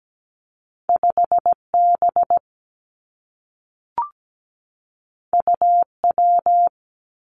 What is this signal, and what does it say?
Morse code, heard as a keyed tone, about 17 wpm, 700 Hz. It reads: 5B UW
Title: Two Challenging Letter-Number Combinations 17wpm